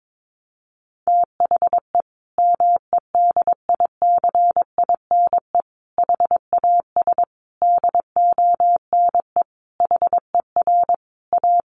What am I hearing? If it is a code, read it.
Morse code, heard as a keyed tone, about 22 wpm, 700 Hz. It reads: THE MEDICINE HAS DONE HER A
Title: Sentences from Top 900 Words 22wpm